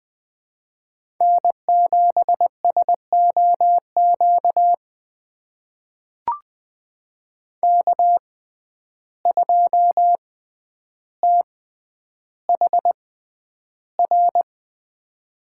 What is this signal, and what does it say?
Morse code, heard as a keyed tone, about 20 wpm, spaced as for 10 wpm, 700 Hz. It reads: N7SOQ K2THR
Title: USA Call Signs - 5 Characters - 6x Character Spacing - 20wpm